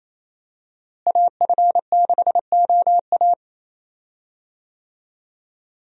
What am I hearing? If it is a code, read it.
Morse code, heard as a keyed tone, about 28 wpm, 700 Hz. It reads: AF6OA
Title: USA Call Signs - 5 Characters - Repeated 2x - 28wpm